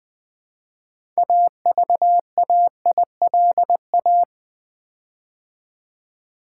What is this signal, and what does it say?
Morse code, heard as a keyed tone, about 20 wpm, 700 Hz. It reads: AVAILA
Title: Build A Word - 6 Letters At A Time - 12-Letter Words - 20wpm